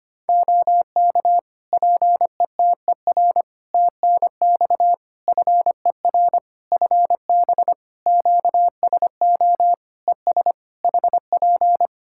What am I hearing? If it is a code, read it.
Morse code, heard as a keyed tone, about 25 wpm, 700 Hz. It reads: OK PETER TNX FER FB QSO ES HP
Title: Practice QSO - Exchange 4 - Ending - 25wpm